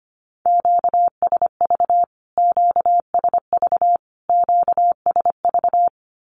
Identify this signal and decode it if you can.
Morse code, heard as a keyed tone, about 25 wpm, 700 Hz. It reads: QSV QSV QSV
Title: Q-Codes Repeated 3x 25wpm